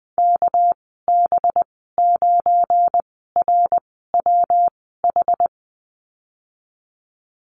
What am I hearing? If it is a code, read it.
Morse code, heard as a keyed tone, about 20 wpm, spaced as for 15 wpm, 700 Hz. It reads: KB9RWH
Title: 231 Lesson 37g - Letter x - Call Signs 20wpm